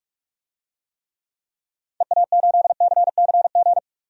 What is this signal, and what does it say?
Morse code, heard as a keyed tone, about 42 wpm, 700 Hz. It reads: EA8CCG
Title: DXCC-100 Call Signs - 6 Characters - 45wpm